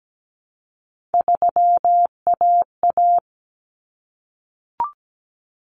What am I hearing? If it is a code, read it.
Morse code, heard as a keyed tone, about 17 wpm, 700 Hz. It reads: 3AA
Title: Common Cut Numbers - 3-Digit 17wpm